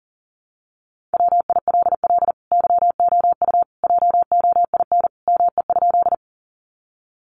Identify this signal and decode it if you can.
Morse code, heard as a keyed tone, about 40 wpm, 700 Hz. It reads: WILL YOU JOIN ME?
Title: Sentences from Top 1400 Words 40wpm